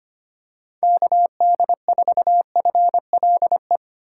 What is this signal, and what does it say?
Morse code, heard as a keyed tone, about 25 wpm, 700 Hz. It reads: KD4FLE
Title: USA Call Signs - 6 Characters - 2x Character Spacing - 25wpm